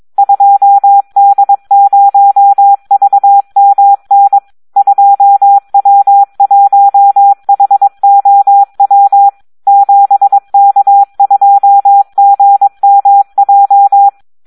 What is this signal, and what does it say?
Morse code, heard as a keyed tone, about 22 wpm, 800 Hz, with radio noise. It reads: 2D0VMN 2W1HOW 7K2GMJ
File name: Audio practica CW / SD 1 22